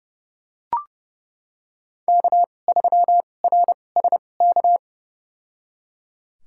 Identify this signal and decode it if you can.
Morse code, heard as a keyed tone, about 30 wpm, spaced as for 22 wpm, 700 Hz. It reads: K3RSK
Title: USA Call Signs - 5 Characters - 2x Character Spacing - 30wpm